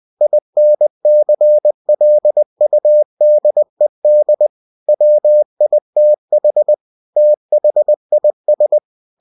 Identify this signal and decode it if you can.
Morse code, heard as a keyed tone, about 20 wpm, 600 Hz. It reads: INCLUDED WITH THIS
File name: winnie-20wpm